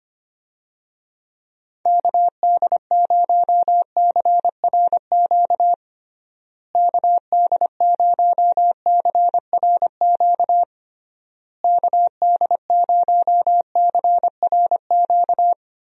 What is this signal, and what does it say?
Morse code, heard as a keyed tone, about 25 wpm, 700 Hz. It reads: KD0CRQ KD0CRQ KD0CRQ
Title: USA Call Signs - 6 Characters - Repeated 3x - 25wpm